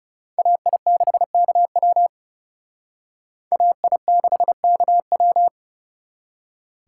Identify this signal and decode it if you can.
Morse code, heard as a keyed tone, about 36 wpm, 700 Hz. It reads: AI6KW AI6KW
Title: Speed Racing - USA 2x2 Call Signs 30wpm